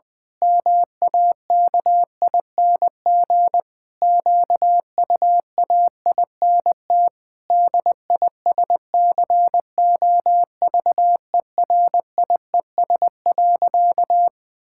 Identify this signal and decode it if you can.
Morse code, heard as a keyed tone, about 19 wpm, 700 Hz. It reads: MAKING QUAINT DISCOVERIES.
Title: Pangram Challenge v2 - 20wpm